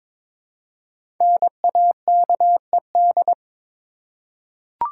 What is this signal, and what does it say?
Morse code, heard as a keyed tone, about 22 wpm, 700 Hz. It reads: NAKED
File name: 5 Letter Words - First 3 Letters Given - 22wpm